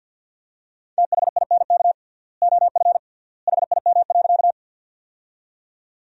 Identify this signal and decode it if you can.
Morse code, heard as a keyed tone, about 50 wpm, 700 Hz. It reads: THINK OF HIM.
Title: 3x Word-Spacing - Sentences from Top 200 Words - 50wpm